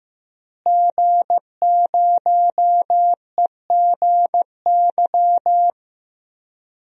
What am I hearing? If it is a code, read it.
Morse code, heard as a keyed tone, about 15 wpm, 700 Hz. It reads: G0EGY
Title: DXCC-100 Call Signs - 5 Characters - Repeated 3x - 15wpm